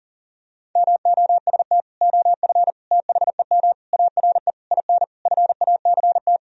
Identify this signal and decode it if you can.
Morse code, heard as a keyed tone, about 40 wpm, 700 Hz. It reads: MOST OF THEM ARE IN FACT
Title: Sentences from Top 1000 Words - Review Part 4 - 40wpm